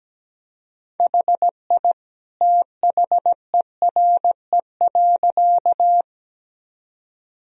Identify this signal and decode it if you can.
Morse code, heard as a keyed tone, about 17 wpm, 700 Hz. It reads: HI THERE.
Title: Sentences from Top 1000 Words - Review Part 3 - 17wpm